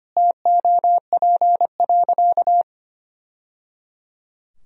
Morse code, heard as a keyed tone, about 25 wpm, 700 Hz. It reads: TOP.
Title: Sentences from Top 1000 Words - Review Part 2 - 25wpm